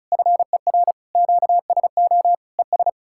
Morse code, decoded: FER QSO ES

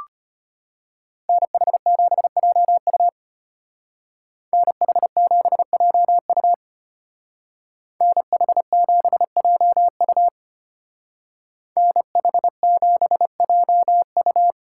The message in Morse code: NH7JU NH7JU NH7JU NH7JU

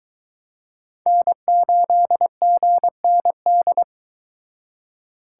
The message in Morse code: N8GND

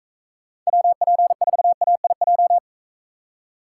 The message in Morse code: WP4AIJ

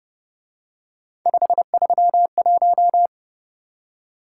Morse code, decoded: 531

